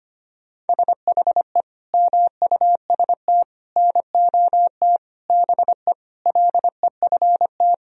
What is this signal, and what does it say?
Morse code, heard as a keyed tone, about 25 wpm, 700 Hz. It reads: SHE MUST NOT BE LEFT
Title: Sentences from Top 1000 Words - Review Part 1 - 25wpm